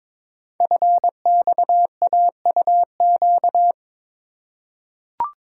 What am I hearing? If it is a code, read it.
Morse code, heard as a keyed tone, about 23 wpm, 700 Hz. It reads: FXAUQ
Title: Code Groups - 5 Letters - 22wpm